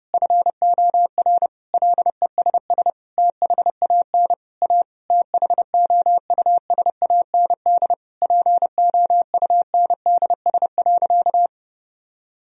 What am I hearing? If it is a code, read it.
Morse code, heard as a keyed tone, about 30 wpm, 700 Hz. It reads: FOR LESS THAN A THOUSAND POUNDS.
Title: Sentences from Top 1000 Words - Review Part 3 - 30wpm